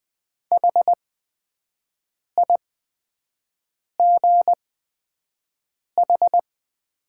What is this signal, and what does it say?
Morse code, heard as a keyed tone, about 20 wpm, spaced as for 5 wpm, 700 Hz. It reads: HIGH
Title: 8x Character Spacing - Top 200 Words - 20wpm